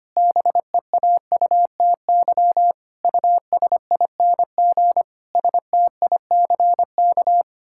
Morse code, heard as a keyed tone, about 25 wpm, 700 Hz. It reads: BEAUTY USING STICK
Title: Pangram Challenge v2 - 25wpm